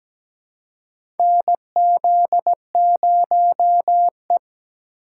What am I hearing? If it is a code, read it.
Morse code, heard as a keyed tone, about 17 wpm, 700 Hz. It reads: NZ0E
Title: USA Call Signs - 4 Characters - 4x Character Spacing - 17wpm